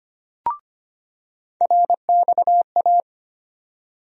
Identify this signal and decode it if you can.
Morse code, heard as a keyed tone, about 25 wpm, 700 Hz. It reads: RXA
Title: USA Call Sign Postfixes - 3 Letters 25wpm